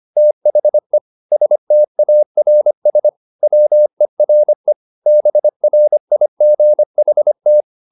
Morse code, heard as a keyed tone, about 25 wpm, 600 Hz. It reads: THE STARS WERE BRIGHT